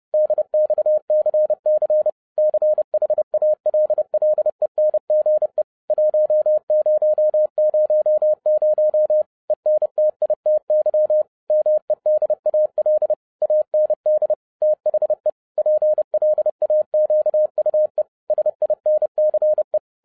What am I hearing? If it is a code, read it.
Morse code, heard as a keyed tone, about 30 wpm, 610 Hz. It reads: DXCC CHALLENGE 1000 ENTITY MEDAL AND THE PLAQUE SINCE
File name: 30wpm